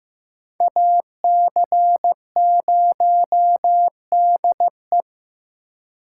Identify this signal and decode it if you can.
Morse code, heard as a keyed tone, about 18 wpm, 700 Hz. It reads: AC0DE
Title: Speed Racing - USA Call Signs 15wpm